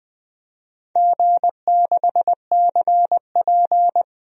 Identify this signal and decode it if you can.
Morse code, heard as a keyed tone, about 20 wpm, 700 Hz. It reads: G6CP
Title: DXCC-100 Call Signs - 4 Characters - 20wpm